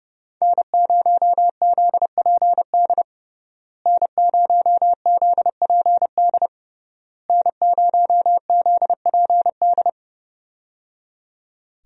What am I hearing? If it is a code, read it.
Morse code, heard as a keyed tone, about 30 wpm, 700 Hz. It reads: N0ZPD N0ZPD N0ZPD
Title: USA Call Signs - 5 Characters - Repeated 3x - 30wpm